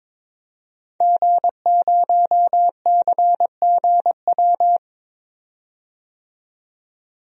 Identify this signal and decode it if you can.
Morse code, heard as a keyed tone, about 21 wpm, 700 Hz. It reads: G0CGW
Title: DXCC-100 Call Signs - 5 Characters - 22wpm